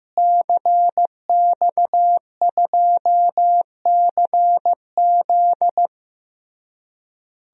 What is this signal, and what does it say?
Morse code, heard as a keyed tone, about 15 wpm, 700 Hz. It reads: CX2CZ
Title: DXCC-100 Call Signs - 5 Characters - Repeated 3x - 15wpm